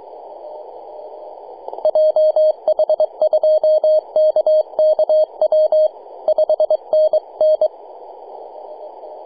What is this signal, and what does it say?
Morse code, heard as a keyed tone, about 22 wpm, 620 Hz. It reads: JH2KKW 5NN
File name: posts / 2015-sept-13-0732z-nh2b-kh0-18-082-cw